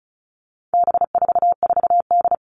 Morse code, heard as a keyed tone, about 34 wpm, 700 Hz. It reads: B44D
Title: All Cut Numbers - 4-Digit 35wpm